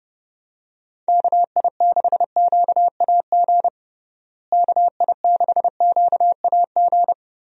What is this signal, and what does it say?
Morse code, heard as a keyed tone, about 30 wpm, 700 Hz. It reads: KI6QAG KI6QAG
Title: USA Call Signs - 6 Characters - Repeated 3x - 30wpm